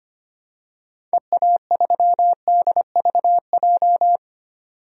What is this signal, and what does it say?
Morse code, heard as a keyed tone, about 25 wpm, 700 Hz. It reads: EA3DVJ
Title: DXCC-100 Call Signs - 6 Characters - 25wpm